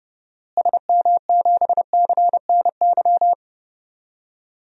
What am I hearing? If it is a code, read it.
Morse code, heard as a keyed tone, about 30 wpm, 700 Hz. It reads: SM7CNY
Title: DXCC-100 Call Signs - 6 Characters - Repeated 3x - 30wpm